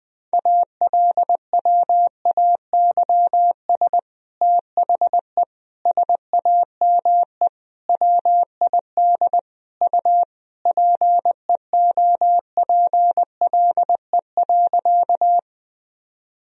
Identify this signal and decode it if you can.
Morse code, heard as a keyed tone, about 20 wpm, 700 Hz. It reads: ALWAYS THE SAME WID U PEOPLE.
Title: Sentences from Top 300 Words and Abbreviations 20wpm